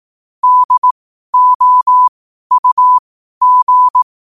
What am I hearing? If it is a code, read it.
Morse code, heard as a keyed tone, about 18 wpm, spaced as for 12 wpm, 1000 Hz. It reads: DOUG
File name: qotdfiles / 12wpm